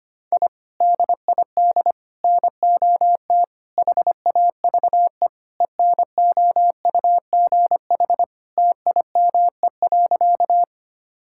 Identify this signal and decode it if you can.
Morse code, heard as a keyed tone, about 25 wpm, 700 Hz. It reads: I DID NOT HAVE ENOUGH TIME.
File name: practice / 25wpm / Sentences with 4-Word Phrases - 25wpm